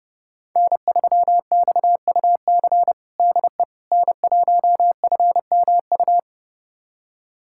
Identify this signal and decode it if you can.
Morse code, heard as a keyed tone, about 32 wpm, 700 Hz. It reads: N3XUC DE N1FMU
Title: USA Callsign DE Callsign - 5 Characters - 30wpm